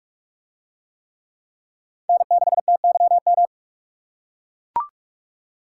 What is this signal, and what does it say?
Morse code, heard as a keyed tone, about 44 wpm, 700 Hz. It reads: N6TYM